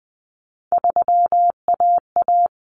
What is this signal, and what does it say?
Morse code, heard as a keyed tone, about 20 wpm, 700 Hz. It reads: 3AA
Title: Common Cut Numbers - 3-Digit 20wpm